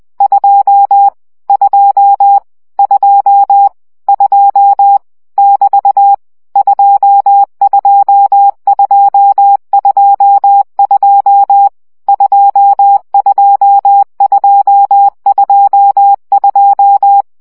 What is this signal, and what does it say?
Morse code, heard as a keyed tone, about 20 wpm, 800 Hz. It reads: 2 2 2 2 <BT> 22222 22222
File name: CW gyakorlas 2 17 baud